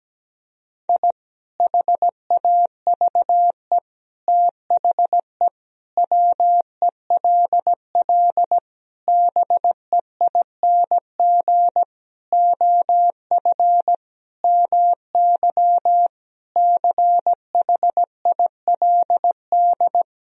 Morse code, heard as a keyed tone, about 17 wpm, 700 Hz. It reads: I HAVE THE WELL BEING OF MY CHILD
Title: Sentences from Top 1000 Words - Review Part 3 - 17wpm